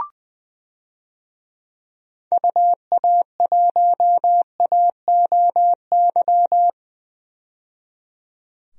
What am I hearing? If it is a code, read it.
Morse code, heard as a keyed tone, about 20 wpm, 700 Hz. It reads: UA1AOY
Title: DXCC-100 Call Signs - 6 Characters - 20wpm